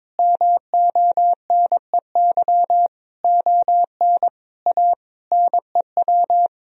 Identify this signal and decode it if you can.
Morse code, heard as a keyed tone, about 22 wpm, 700 Hz. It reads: MONEY ON A NEW
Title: Sentences from Top 1000 Words - Review Part 4 - 22wpm